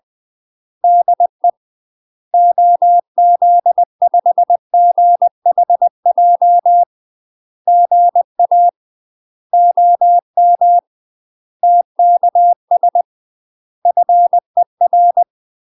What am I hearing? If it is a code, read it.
Morse code, heard as a keyed tone, about 20 wpm, 700 Hz. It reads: DE OZ5GHJ GA OM TKS FER
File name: UK MASTER QSOs TRAINING / 20WPM / UK MASTER QSOs 0008